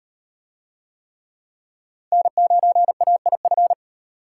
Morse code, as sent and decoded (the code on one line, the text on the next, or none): -. ----. .- .. ..-.
N9AIF